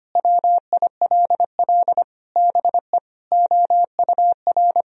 Morse code, heard as a keyed tone, about 25 wpm, 700 Hz. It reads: WILL BE OUR
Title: Sentences from Top 1000 Words - Review Part 3 - 25wpm